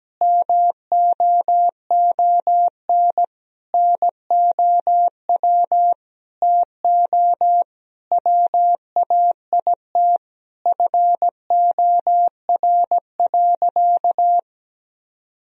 Morse code, heard as a keyed tone, about 17 wpm, 700 Hz. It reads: MOON NOW TO WAIT FOR.
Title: Sentences from Top 1000 Words - Review Part 3 - 17wpm